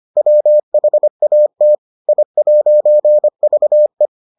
Morse code, heard as a keyed tone, about 25 wpm, 600 Hz. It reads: WHAT I'VE